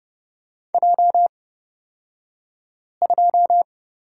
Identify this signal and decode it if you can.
Morse code, heard as a keyed tone, about 30 wpm, spaced as for 5 wpm, 700 Hz. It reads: J2
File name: Lesson 38d - Letter q - Two Characters 30wpm